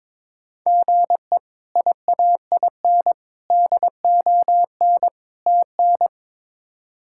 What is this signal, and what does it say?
Morse code, heard as a keyed tone, about 22 wpm, 700 Hz. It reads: GE IAIN DON TN
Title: K1USN SST QSO Practice - 22wpm